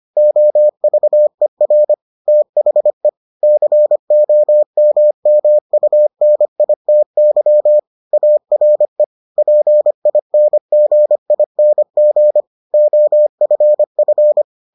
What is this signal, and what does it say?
Morse code, heard as a keyed tone, about 25 wpm, 600 Hz. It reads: OVER THE COMMUNITY ARE PINGING OFF